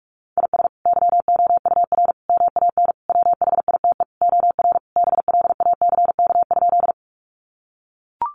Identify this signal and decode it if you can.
Morse code, heard as a keyed tone, about 44 wpm, 700 Hz. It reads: IS YOUR MAN WHITE OR BLACK?
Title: Sentences from Top 400 Words 45wpm